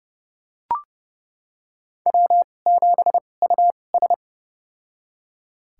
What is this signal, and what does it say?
Morse code, heard as a keyed tone, about 30 wpm, spaced as for 22 wpm, 700 Hz. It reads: W7US
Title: USA Call Signs - 4 Characters - 2x Character Spacing - 30wpm